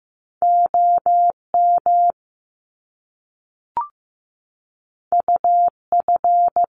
Morse code, heard as a keyed tone, about 15 wpm, 700 Hz. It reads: OM UF
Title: Top Two Letter Combinations in Words 15wpm